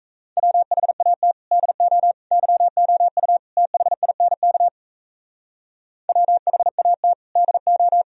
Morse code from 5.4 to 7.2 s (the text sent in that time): WHAT